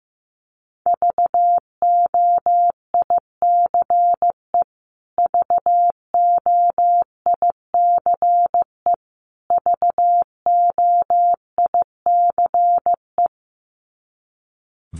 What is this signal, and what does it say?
Morse code, heard as a keyed tone, about 15 wpm, 700 Hz. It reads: VOICE VOICE VOICE
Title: Top 500 Words Repeated 15wpm